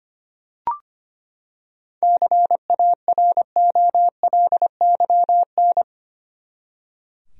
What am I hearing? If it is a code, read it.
Morse code, heard as a keyed tone, about 25 wpm, 700 Hz. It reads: CAROLYN
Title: USA Common Names - 25wpm